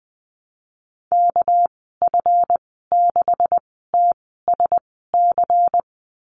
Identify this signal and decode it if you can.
Morse code, heard as a keyed tone, about 20 wpm, spaced as for 14 wpm, 700 Hz. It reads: KF6TSC
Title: 200 Lesson 32g - Letter k - Call Signs 20wpm